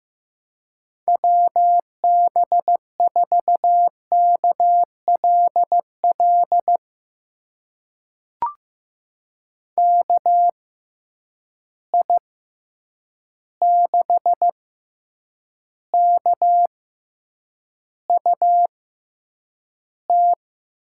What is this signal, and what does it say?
Morse code, heard as a keyed tone, about 15 wpm, spaced as for 8 wpm, 700 Hz. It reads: WB4KLL KI6KUT
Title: USA Call Signs - 6 Characters - 6x Character Spacing - 15wpm